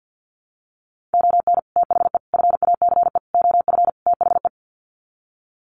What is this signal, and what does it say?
Morse code, heard as a keyed tone, about 48 wpm, 700 Hz. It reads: ON THE FACE OF THE
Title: Sets of 5 Words Encore2 50wpm